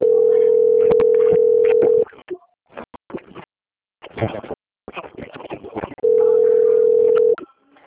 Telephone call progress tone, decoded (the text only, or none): ringback tone